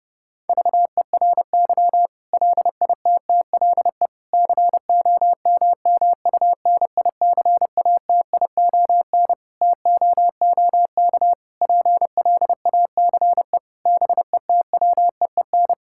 Morse code, heard as a keyed tone, about 29 wpm, 700 Hz. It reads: VERY LITTLE COMMUNICATION TOOK PLACE BETWEEN
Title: Sentences from Top 1000 Words - Review Part 6 - 30wpm